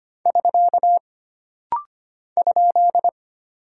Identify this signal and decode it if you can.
Morse code, heard as a keyed tone, about 25 wpm, 700 Hz. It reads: <SK> ?
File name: Single Punctuation-Prosign - Rapid-Fire - 25wpm